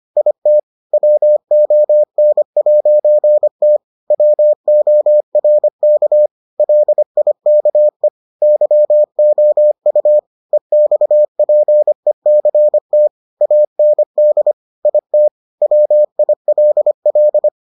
Morse code, heard as a keyed tone, about 25 wpm, 600 Hz. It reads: IT WON'T WORK LIKE YOU EXPECT AND IT WILL